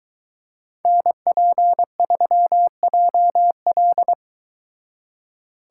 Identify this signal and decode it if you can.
Morse code, heard as a keyed tone, about 22 wpm, 700 Hz. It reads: NP3JL